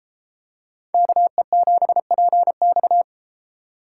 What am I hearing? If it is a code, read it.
Morse code, heard as a keyed tone, about 33 wpm, 700 Hz. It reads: KE7PX